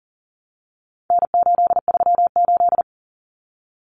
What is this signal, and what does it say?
Morse code, heard as a keyed tone, about 40 wpm, 700 Hz. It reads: N838
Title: Common Cut Numbers - 4-Digit 40wpm